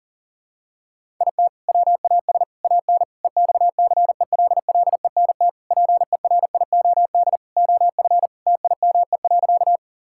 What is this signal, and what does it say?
Morse code, heard as a keyed tone, about 40 wpm, 700 Hz. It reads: IT WAS AN EXCELLENT PERIOD OF TIME.
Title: Sentences from Top 1000 Words - Review Part 5 - 40wpm